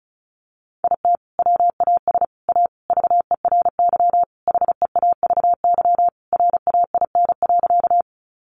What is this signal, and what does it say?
Morse code, heard as a keyed tone, about 34 wpm, 700 Hz. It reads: IT WAS A VERY HEAVY RAIN.
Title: Sentences from Top 1100 Words 35wpm